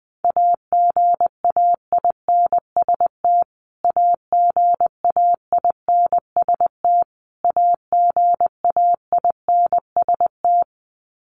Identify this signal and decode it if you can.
Morse code, heard as a keyed tone, about 20 wpm, 700 Hz. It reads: AGAINST AGAINST AGAINST
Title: Top 500 Words Repeated 20wpm